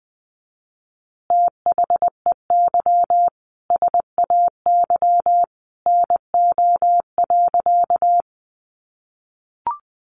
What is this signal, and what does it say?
Morse code, heard as a keyed tone, about 20 wpm, 700 Hz. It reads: THEY SAY NO.
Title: Easy Phrases 20wpm